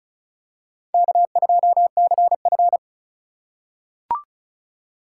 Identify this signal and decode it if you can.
Morse code, heard as a keyed tone, about 36 wpm, 700 Hz. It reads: K2CF
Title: USA Call Signs - 4 Characters - 3x Character Spacing - 35wpm